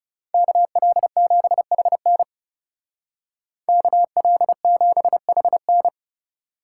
Morse code, heard as a keyed tone, about 39 wpm, 700 Hz. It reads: KL7HN KL7HN